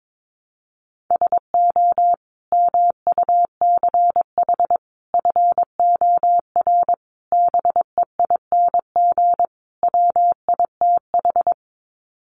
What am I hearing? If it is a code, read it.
Morse code, heard as a keyed tone, about 22 wpm, 700 Hz. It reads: SO MUCH FOR BEING WITH
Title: Sets of 5 Words Encore 22wpm